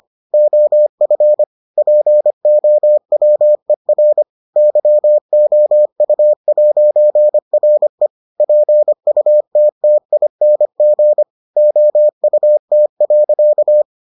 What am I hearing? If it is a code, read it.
Morse code, heard as a keyed tone, about 25 wpm, 600 Hz. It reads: OF POWER YOU'RE PUTTING OUT.